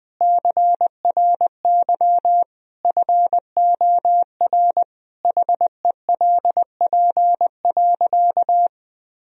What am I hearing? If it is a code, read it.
Morse code, heard as a keyed tone, about 20 wpm, 700 Hz. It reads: CRY FOR HELP.
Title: Sentences from Top 1000 Words - Review Part 3 - 20wpm